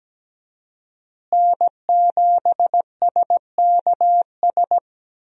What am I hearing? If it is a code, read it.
Morse code, heard as a keyed tone, about 17 wpm, 700 Hz. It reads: N7SKS
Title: USA Call Signs - 5 Characters - Repeated 2x - 17wpm